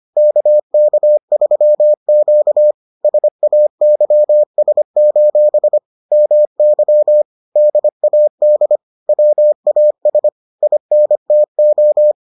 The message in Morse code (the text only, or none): KK3Q SAYS: MY DAD WAS INTO